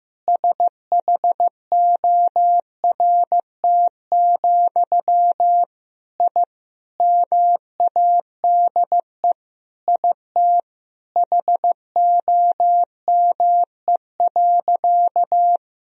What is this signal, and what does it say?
Morse code, heard as a keyed tone, about 15 wpm, 700 Hz. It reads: SHORT, I MADE IT HOME.
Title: Sentences from Top 1000 Words - Review Part 2 - 15wpm